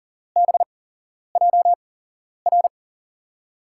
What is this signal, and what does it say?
Morse code, heard as a keyed tone, about 40 wpm, spaced as for 11 wpm, 700 Hz. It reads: BJR
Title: USA Call Signs - 5 Characters - 8x Character Spacing - 40wpm